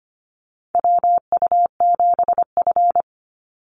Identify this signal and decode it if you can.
Morse code, heard as a keyed tone, about 25 wpm, 700 Hz. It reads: WU7F
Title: Call Signs - 25wpm